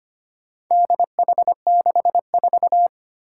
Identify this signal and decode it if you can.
Morse code, heard as a keyed tone, about 25 wpm, 700 Hz. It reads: DH64
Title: Four Challenging Letter-Number Combinations 25wpm